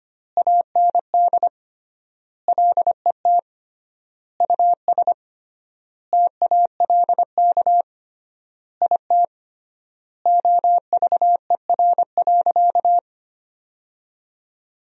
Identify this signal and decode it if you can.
Morse code, heard as a keyed tone, about 25 wpm, 700 Hz. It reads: AND LET US TALK IT OVER.
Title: 3x Word-Spacing - Sentences from Top 500 Words - 25wpm